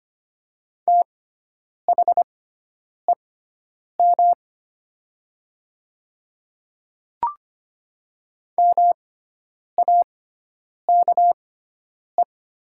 Morse code, heard as a keyed tone, about 25 wpm, spaced as for 9 wpm, 700 Hz. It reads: THEM MAKE
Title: 6x Character Spacing - Top 100 Words - 25wpm